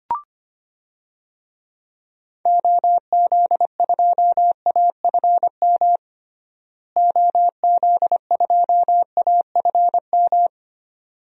Morse code, heard as a keyed tone, about 25 wpm, 700 Hz. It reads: OZ2AFM OZ2AFM
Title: DXCC-100 Call Signs - 6 Characters - Repeated 2x - 25wpm